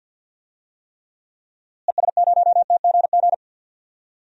Morse code, heard as a keyed tone, about 48 wpm, 700 Hz. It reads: ES0TGG
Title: DXCC-100 Call Signs - 6 Characters - Repeated 3x - 50wpm